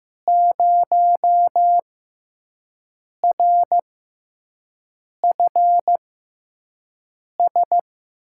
Morse code, heard as a keyed tone, about 15 wpm, spaced as for 5 wpm, 700 Hz. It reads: 0RFS
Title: USA Call Signs - 5 Characters - 6x Character Spacing - 15wpm